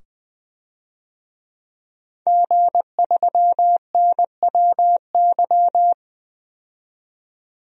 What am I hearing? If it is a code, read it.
Morse code, heard as a keyed tone, about 20 wpm, 700 Hz. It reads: G3NWY
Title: DXCC-100 Call Signs - 5 Characters - 20wpm